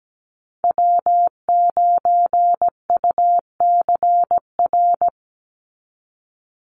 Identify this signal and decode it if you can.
Morse code, heard as a keyed tone, about 17 wpm, 700 Hz. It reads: W9UCR
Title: Call Signs Repeated 3x - 17wpm